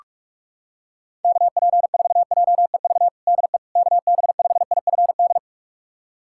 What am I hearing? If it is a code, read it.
Morse code, heard as a keyed tone, about 47 wpm, 690 Hz. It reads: KP4JEV DE KB5IFD
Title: USA Callsign DE Callsign 45wpm